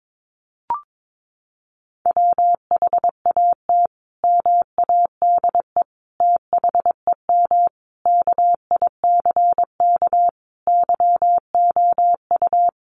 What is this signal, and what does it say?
Morse code, heard as a keyed tone, about 22 wpm, 700 Hz. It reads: WHAT MADE THEM KICK YOU
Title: Sentences from Top 1300 Words 22wpm